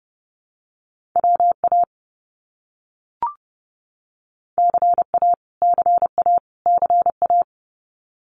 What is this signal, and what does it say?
Morse code, heard as a keyed tone, about 30 wpm, 700 Hz. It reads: WA CA CA CA
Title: US States Repeated 3x 30wpm